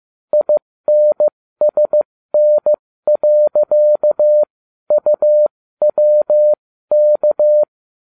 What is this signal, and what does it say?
Morse code, heard as a keyed tone, about 15 wpm, spaced as for 13 wpm, 600 Hz. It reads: INSN.UWK